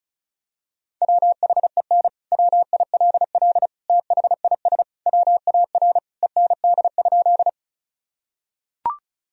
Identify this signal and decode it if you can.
Morse code, heard as a keyed tone, about 36 wpm, 700 Hz. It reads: WHEN WILL THIS WAR END?
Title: Sentences from Top 1000 Words - Review Part 2 - 35wpm